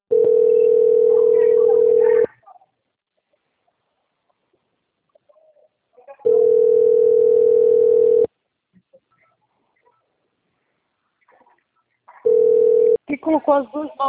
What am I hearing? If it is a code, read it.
Ringback tone (North America), 440 + 480 Hz, with speech.
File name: out- - - - - .84548